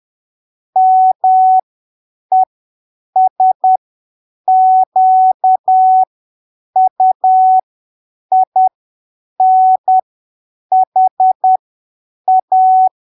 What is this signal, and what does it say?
Morse code, heard as a keyed tone, about 10 wpm, spaced as for 7 wpm, 750 Hz. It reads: MESQUINHA
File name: arquivos / texto / texto 27 10ppm